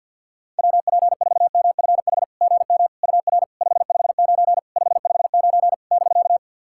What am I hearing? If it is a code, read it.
Morse code, heard as a keyed tone, about 53 wpm, 660 Hz. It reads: WP4MFH GM UR 559 559 <BK>
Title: SOTA QSO Practice 50wpm